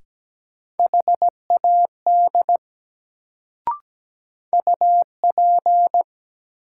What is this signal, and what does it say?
Morse code, heard as a keyed tone, about 17 wpm, 700 Hz. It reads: HAD UP
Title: Sentences from Top 100 Words - New Word Drill - 17wpm